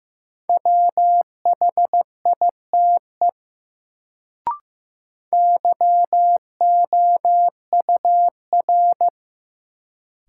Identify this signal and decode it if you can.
Morse code, heard as a keyed tone, about 15 wpm, 700 Hz. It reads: WHITE YOUR
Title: Sets of 2 Words - New Word Drill - 15wpm